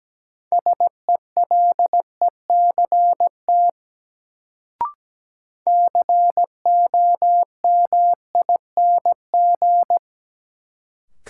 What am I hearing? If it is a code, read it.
Morse code, heard as a keyed tone, about 17 wpm, 700 Hz. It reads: SELECT COMING
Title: Sentences from Top 1300 Words - New Word Drill - 17wpm